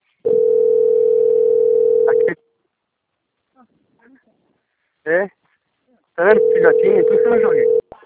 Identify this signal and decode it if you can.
Ringback tone (North America), 440 + 480 Hz, with speech.
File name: out- - - - - .16362